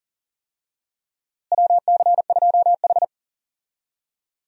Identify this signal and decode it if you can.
Morse code, heard as a keyed tone, about 40 wpm, 700 Hz. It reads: WC2H
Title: USA Call Signs - 4 Characters - Repeated 2x - 40wpm